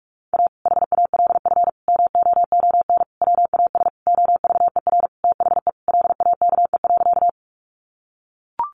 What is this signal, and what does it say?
Morse code, heard as a keyed tone, about 44 wpm, 700 Hz. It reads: A HALF MOON WAS OVER THE LAKE.
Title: Sentences from Top 1100 Words 45wpm